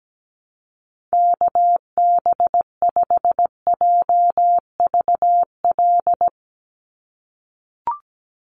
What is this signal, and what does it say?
Morse code, heard as a keyed tone, about 17 wpm, 700 Hz. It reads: KB5JVL
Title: USA Call Signs 17wpm